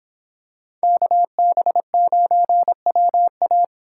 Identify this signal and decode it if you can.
Morse code, heard as a keyed tone, about 25 wpm, 700 Hz. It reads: KB9WA